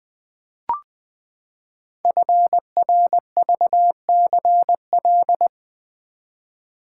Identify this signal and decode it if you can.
Morse code, heard as a keyed tone, about 20 wpm, 700 Hz. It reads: FRVCL
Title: Code Groups - 5 Letters - 20wpm